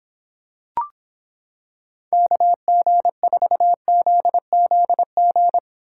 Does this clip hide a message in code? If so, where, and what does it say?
Yes: Morse code from 2.1 to 5.6 s, heard as a keyed tone, about 26 wpm, 700 Hz. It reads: KG4ZZG